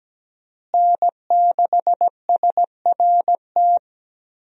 Morse code, heard as a keyed tone, about 17 wpm, 700 Hz. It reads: N6SRT
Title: USA Call Signs - 5 Characters - Repeated 2x - 17wpm